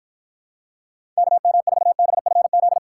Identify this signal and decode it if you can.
Morse code, heard as a keyed tone, about 53 wpm, 700 Hz. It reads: KM4BFZ